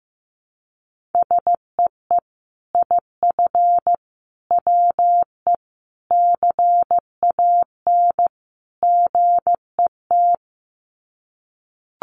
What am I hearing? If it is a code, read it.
Morse code, heard as a keyed tone, about 15 wpm, 700 Hz. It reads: SEE IF WE CAN GET
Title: Sets of 5 Words Encore2 15wpm